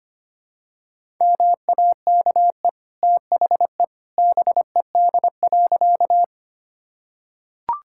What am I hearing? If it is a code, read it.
Morse code, heard as a keyed tone, about 25 wpm, 700 Hz. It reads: MAKE THE BED.
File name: Easy Sentences with Hints - 25wpm